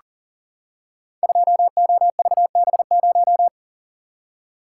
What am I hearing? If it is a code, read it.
Morse code, heard as a keyed tone, about 39 wpm, 700 Hz. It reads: 2OVB0
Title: Code Groups - 5 Letter:Numbers - 40wpm